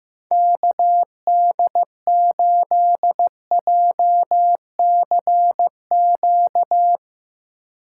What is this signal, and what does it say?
Morse code, heard as a keyed tone, about 16 wpm, 700 Hz. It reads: KD8JCQ
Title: USA Call Signs - 6 Characters - 3x Character Spacing - 15wpm